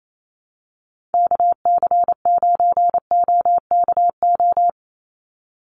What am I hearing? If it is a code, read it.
Morse code, heard as a keyed tone, about 28 wpm, 700 Hz. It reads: KC9OKO
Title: USA Call Signs 28wpm